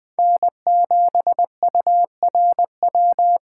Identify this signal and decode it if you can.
Morse code, heard as a keyed tone, about 22 wpm, 700 Hz. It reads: N7URW